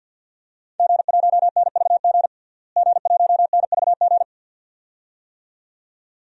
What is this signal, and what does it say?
Morse code, heard as a keyed tone, about 48 wpm, 700 Hz. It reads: G1NVG G1NVG
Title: DXCC-100 Call Signs - 5 Characters - Repeated 2x - 50wpm